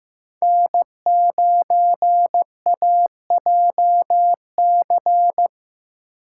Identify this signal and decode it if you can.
Morse code, heard as a keyed tone, about 15 wpm, 700 Hz. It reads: N9AJC
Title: USA Call Signs - 5 Characters - 6x Character Spacing - 15wpm